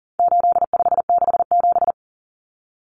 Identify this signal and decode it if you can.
Morse code, heard as a keyed tone, about 40 wpm, 700 Hz. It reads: 8567
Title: Numbers - 4-Digit 40wpm